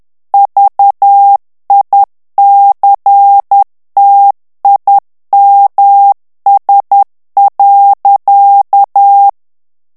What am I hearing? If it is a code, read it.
Morse code, heard as a keyed tone, about 10 wpm, 800 Hz. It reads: VICTIMS.
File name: morse mp3 / ANGLAIS texte clair 2 vitesse 10 mots minute